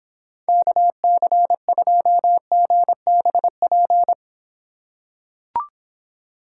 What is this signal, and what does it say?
Morse code, heard as a keyed tone, about 25 wpm, 700 Hz. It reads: KC2GBP